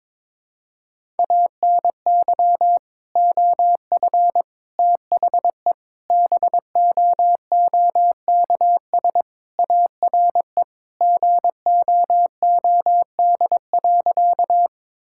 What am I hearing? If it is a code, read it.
Morse code, heard as a keyed tone, about 22 wpm, 700 Hz. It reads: ANY OF THE BOOKS ARE GOOD.
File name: practice / 22wpm / Sentences with 3-Word Phrases - 22wpm